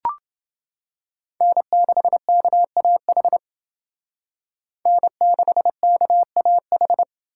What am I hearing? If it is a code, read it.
Morse code, heard as a keyed tone, about 29 wpm, 700 Hz. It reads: N6KAH N6KAH